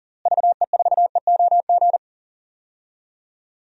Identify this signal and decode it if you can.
Morse code, heard as a keyed tone, about 37 wpm, 700 Hz. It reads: VE4EOG
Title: DXCC-100 Call Signs - 6 Characters - Repeated 3x - 40wpm